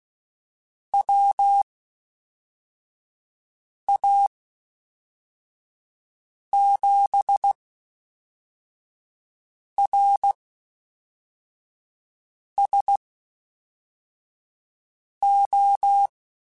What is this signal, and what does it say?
Morse code, heard as a keyed tone, about 16 wpm, spaced as for 4 wpm, 800 Hz. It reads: WA7RSO
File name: Misc / Misc-mp3s / QST QST @5